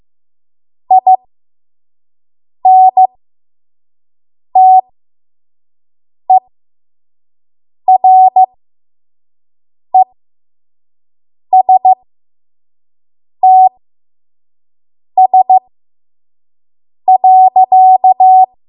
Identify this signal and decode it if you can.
Morse code, heard as a keyed tone, about 15 wpm, spaced as for 5 wpm, 750 Hz. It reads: INTERESTS.
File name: Morse / 05WPM